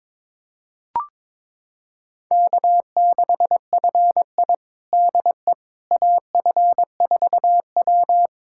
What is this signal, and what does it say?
Morse code, heard as a keyed tone, about 22 wpm, 700 Hz. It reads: K6FI DE AF4W
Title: USA Callsign DE Callsign - 4 Characters - 22wpm